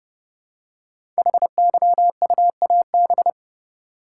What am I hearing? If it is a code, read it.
Morse code, heard as a keyed tone, about 28 wpm, 700 Hz. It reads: HYUAB
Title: Code Groups - 5 Letter:Numbers - 30wpm